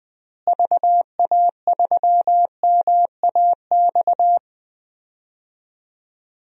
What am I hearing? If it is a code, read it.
Morse code, heard as a keyed tone, about 21 wpm, 700 Hz. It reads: VA3MAX
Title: DXCC-100 Call Signs - 6 Characters - Repeated 2x - 20wpm